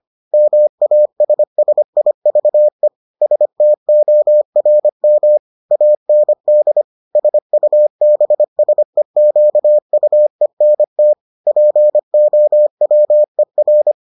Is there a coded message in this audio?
Yes: Morse code from 0.3 to 13.9 s, heard as a keyed tone, about 25 wpm, 600 Hz. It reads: MASSIVE STORM AND SUBSEQUENT POWER